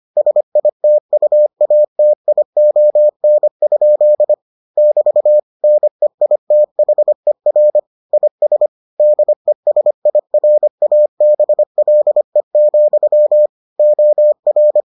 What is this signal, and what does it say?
Morse code, heard as a keyed tone, about 25 wpm, 600 Hz. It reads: SITUATION? <BT> NEITHER IS DESIRABLE, OR